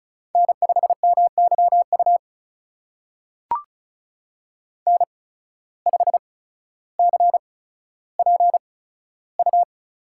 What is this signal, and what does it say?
Morse code, heard as a keyed tone, about 36 wpm, spaced as for 15 wpm, 700 Hz. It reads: N5MYU N5CPU
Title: USA Call Signs - 5 Characters - 8x Character Spacing - 35wpm